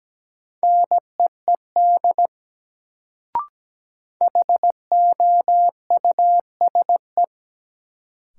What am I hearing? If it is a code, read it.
Morse code, heard as a keyed tone, about 17 wpm, 700 Hz. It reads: NEED HOUSE
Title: Sets of 2 Words - New Word Drill - 17wpm